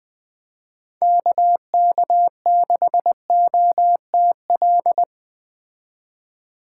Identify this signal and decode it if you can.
Morse code, heard as a keyed tone, about 20 wpm, 700 Hz. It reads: KK6OTL
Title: Speed Racing - USA 2x3 Call Signs 17wpm